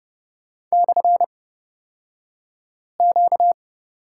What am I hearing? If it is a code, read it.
Morse code, heard as a keyed tone, about 30 wpm, spaced as for 5 wpm, 700 Hz. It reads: /Q
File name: Lesson 38d - Letter q - Two Characters 30wpm